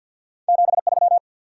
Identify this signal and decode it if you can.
Morse code, heard as a keyed tone, about 53 wpm, 700 Hz. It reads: 73
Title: Top 90 QSO Elements 50wpm